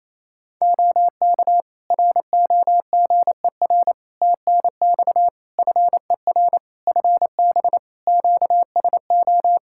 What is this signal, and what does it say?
Morse code, heard as a keyed tone, about 28 wpm, 700 Hz. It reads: OK ROGER TNX FER FB QSO
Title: Practice QSO - Exchange 4 - Ending - 28wpm